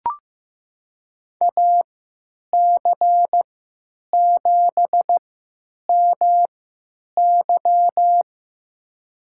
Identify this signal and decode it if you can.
Morse code, heard as a keyed tone, about 15 wpm, spaced as for 9 wpm, 700 Hz. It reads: AC7MY
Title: USA Call Signs - 5 Characters - 3x Character Spacing - 15wpm